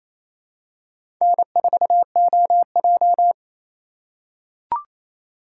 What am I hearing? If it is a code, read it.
Morse code, heard as a keyed tone, about 27 wpm, 700 Hz. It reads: N4OJ
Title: USA Call Signs - 3x Character Spacing - 28wpm